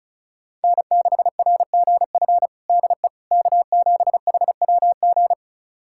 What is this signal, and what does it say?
Morse code, heard as a keyed tone, about 36 wpm, 700 Hz. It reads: N6RGF DE K7HWG
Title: USA Callsign DE Callsign - 5 Characters - 35wpm